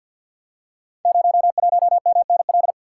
Morse code, heard as a keyed tone, about 49 wpm, 700 Hz. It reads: 01MNL
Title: Code Groups - 5 Letter:Numbers - 50wpm